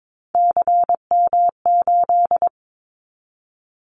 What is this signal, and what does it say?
Morse code, heard as a keyed tone, about 22 wpm, 700 Hz. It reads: CM8
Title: Callsign Prefixes for Top 100 DXCC - 22wpm